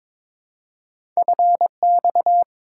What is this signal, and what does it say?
Morse code, heard as a keyed tone, about 23 wpm, 690 Hz. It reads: FX
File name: practice / 22wpm / Two Letter-Number Combinations - 22wpm